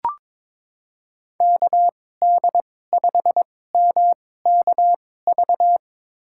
Morse code, heard as a keyed tone, about 22 wpm, spaced as for 16 wpm, 700 Hz. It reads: KD5MKV
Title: USA Call Signs - 2x Character Spacing - 22wpm